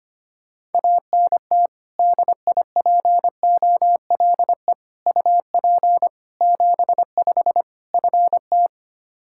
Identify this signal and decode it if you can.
Morse code, heard as a keyed tone, about 25 wpm, 700 Hz. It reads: ANT DIPOLE UP 75 FT
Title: Practice QSO - Exchange 2 - Info - 25wpm